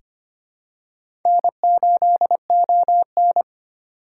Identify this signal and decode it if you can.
Morse code, heard as a keyed tone, about 25 wpm, 700 Hz. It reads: N8ON
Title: USA Call Signs - 4 Characters - 2x Character Spacing - 25wpm